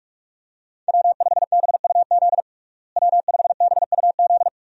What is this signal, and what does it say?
Morse code, heard as a keyed tone, about 46 wpm, 700 Hz. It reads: W5BUZ W5BUZ
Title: USA Call Signs - 5 Characters - Repeated 2x - 45wpm